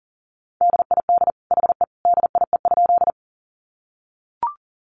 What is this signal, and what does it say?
Morse code, heard as a keyed tone, about 37 wpm, 700 Hz. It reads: DID HE DIE?
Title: Sentences from Top 1300 Words 40wpm